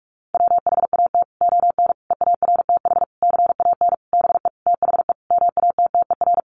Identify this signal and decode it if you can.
Morse code, heard as a keyed tone, about 44 wpm, 700 Hz. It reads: WHAT ON EARTH CAN BE THE MATTER
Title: Sentences from Top 700 Words 45wpm